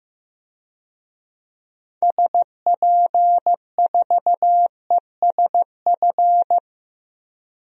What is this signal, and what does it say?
Morse code, heard as a keyed tone, about 15 wpm, 700 Hz. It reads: SP4ESF
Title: DXCC-100 Call Signs - 6 Characters - Repeated 3x - 15wpm